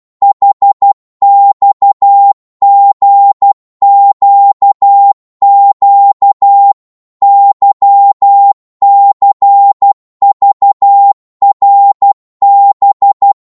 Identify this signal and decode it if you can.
Morse code, heard as a keyed tone, about 12 wpm, 800 Hz. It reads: HXGQQ YCVRB
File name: Morse / ltr-12wpm